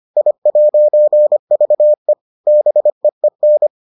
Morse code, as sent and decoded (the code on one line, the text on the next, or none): .. .----. ...- . / -... . -.
I'VE BEEN